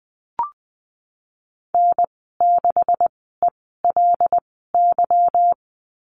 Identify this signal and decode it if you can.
Morse code, heard as a keyed tone, about 20 wpm, spaced as for 14 wpm, 700 Hz. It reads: N6ELY
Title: 161 Lesson 26g - Letter g - Call Signs 20wpm